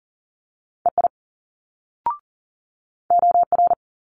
Morse code, heard as a keyed tone, about 40 wpm, 700 Hz. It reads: EI OR
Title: Multi-letter Phonemes Quick Fire 40wpm